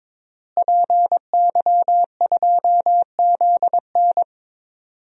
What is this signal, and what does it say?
Morse code, heard as a keyed tone, about 22 wpm, 700 Hz. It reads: PY2ZN
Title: DXCC-100 Call Signs - 5 Characters - Repeated 3x - 22wpm